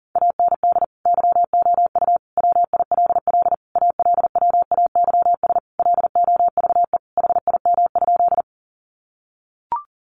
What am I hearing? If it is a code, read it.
Morse code, heard as a keyed tone, about 40 wpm, 700 Hz. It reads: AND YOU WILL ALWAYS LOVE HIM?
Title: Sentences from Top 400 Words 40wpm